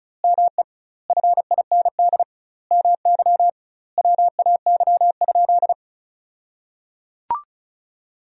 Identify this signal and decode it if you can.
Morse code, heard as a keyed tone, about 36 wpm, 700 Hz. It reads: ME FIND MY WAY?
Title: 2x Word-Spacing - Sentences from Top 200 Words - 35wpm